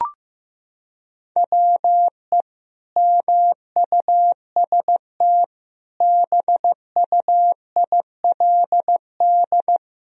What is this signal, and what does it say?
Morse code, heard as a keyed tone, about 15 wpm, 700 Hz. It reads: WE MUST BUILD
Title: Sentences from Top 1000 Words - Review Part 1 - 15wpm